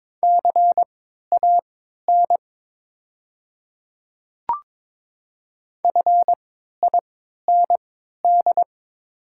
Morse code, heard as a keyed tone, about 22 wpm, spaced as for 13 wpm, 700 Hz. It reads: CAN FIND
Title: 3x Character Spacing - Top 200 Words - 22wpm